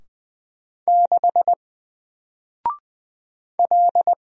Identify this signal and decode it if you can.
Morse code, heard as a keyed tone, about 20 wpm, 700 Hz. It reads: 6 L
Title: Single Character (Letter:Number:Punctuation:Prosign) 20wpm